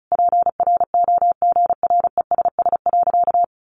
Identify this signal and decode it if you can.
Morse code, heard as a keyed tone, about 34 wpm, 700 Hz. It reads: PROGRESS.
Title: Sentences from Top 1400 Words 35wpm